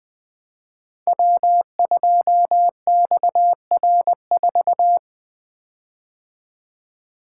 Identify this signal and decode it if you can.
Morse code, heard as a keyed tone, about 20 wpm, 700 Hz. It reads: W2XR4
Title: Code Groups - 5 Letter:Numbers - 20wpm